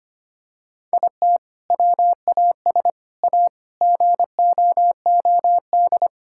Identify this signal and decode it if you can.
Morse code, heard as a keyed tone, about 25 wpm, 700 Hz. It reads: IT WAS A GOOD
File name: practice / 25wpm / Sentences with 2-Word Phrases - 25wpm